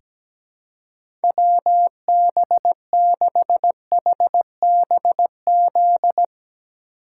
Morse code, heard as a keyed tone, about 17 wpm, 700 Hz. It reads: WB6HBZ
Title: USA Call Signs - 6 Characters - 2x Character Spacing - 17wpm